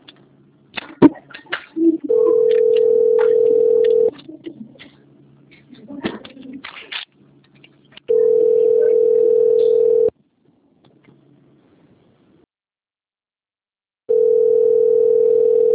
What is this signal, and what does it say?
Ringback tone (North America), 440 + 480 Hz.